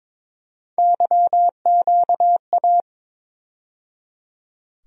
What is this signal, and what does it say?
Morse code, heard as a keyed tone, about 22 wpm, 700 Hz. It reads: YQA
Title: USA Call Sign Postfixes - 3 Letters 22wpm